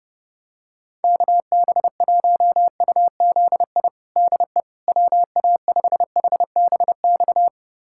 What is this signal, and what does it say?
Morse code, heard as a keyed tone, about 30 wpm, 700 Hz. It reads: KB1UZI DE WA5HBX
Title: USA Callsign DE Callsign 30wpm